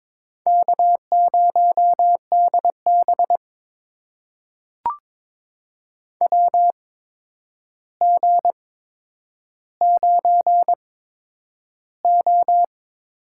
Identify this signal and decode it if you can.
Morse code, heard as a keyed tone, about 22 wpm, spaced as for 9 wpm, 700 Hz. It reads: K0DB WG9O
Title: USA Call Signs - 4 Characters - 8x Character Spacing - 22wpm